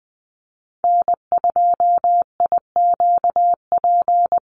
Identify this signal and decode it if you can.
Morse code, heard as a keyed tone, about 20 wpm, 700 Hz. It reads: N2IQP
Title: USA Call Signs Repeated 3x 20wpm